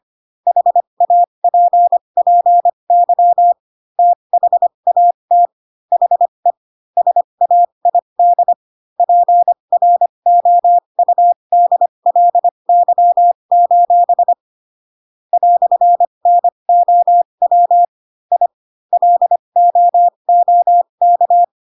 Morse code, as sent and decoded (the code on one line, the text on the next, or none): .... .- .--. .--. -.-- / - .... .- - / .... / ... .- .. -.. / .--. .-. --- ..- -.. .-.. -.-- ---... / .-..-. -. --- .-- / .. / .-.. --- --- -.-
HAPPY THAT HE SAID PROUDLY: "NOW I LOOK